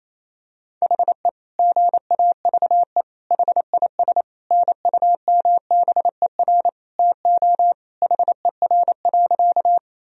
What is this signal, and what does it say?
Morse code, heard as a keyed tone, about 28 wpm, 700 Hz. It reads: HE GAVE HIS NUMBER TO HER.
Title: Sentences from Top 1000 Words - Review Part 1 - 28wpm